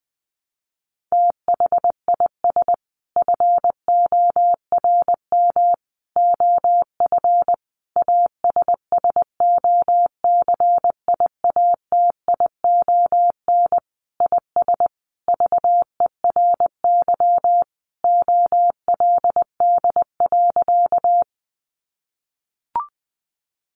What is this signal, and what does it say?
Morse code, heard as a keyed tone, about 20 wpm, 700 Hz. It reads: THIS FORM OF ASSOCIATION IS VERY OLD.
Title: Sentences from Top 1100 Words 20wpm